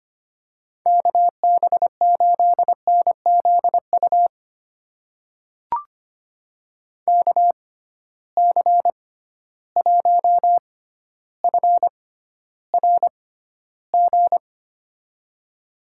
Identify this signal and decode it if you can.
Morse code, heard as a keyed tone, about 25 wpm, spaced as for 13 wpm, 700 Hz. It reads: KB8NZU KC1FRG
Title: USA Call Signs - 6 Characters - 6x Character Spacing - 25wpm